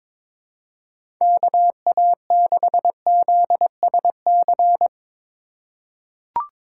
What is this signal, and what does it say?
Morse code, heard as a keyed tone, about 22 wpm, 700 Hz. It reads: KA6ZSC
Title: USA Call Signs - 6 Characters - 3x Character Spacing - 22wpm